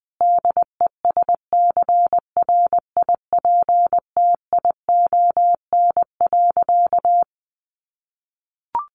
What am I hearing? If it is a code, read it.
Morse code, heard as a keyed tone, about 20 wpm, 700 Hz. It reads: DESCRIPTION.
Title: Sentences from Top 1300 Words 20wpm